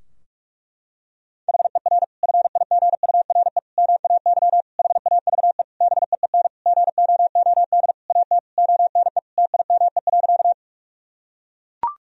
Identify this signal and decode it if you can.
Morse code, heard as a keyed tone, about 41 wpm, 700 Hz. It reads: HER FIGURE MAY HAVE BEEN GOOD AT ONE TIME.
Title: Sentences from Top 1000 Words - Review Part 2 - 45wpm